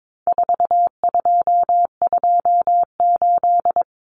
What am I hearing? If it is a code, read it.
Morse code, heard as a keyed tone, about 22 wpm, 700 Hz. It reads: 4228
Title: Numbers - 4-Digit 22wpm